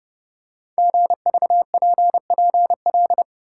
Morse code, heard as a keyed tone, about 30 wpm, 700 Hz. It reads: GVPPL
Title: Code Groups - 5 Letters - 30wpm